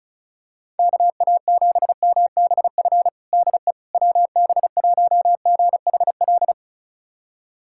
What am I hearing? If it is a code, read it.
Morse code, heard as a keyed tone, about 34 wpm, 700 Hz. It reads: KA7MBF DE WB1GHL
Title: USA Callsign DE Callsign - 6 Characters - 35wpm